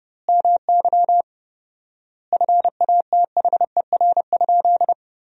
Morse code, heard as a keyed tone, about 32 wpm, 700 Hz. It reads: MY FATHER?
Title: 4x Word-Spacing - Sentences from Top 400 Words - 30wpm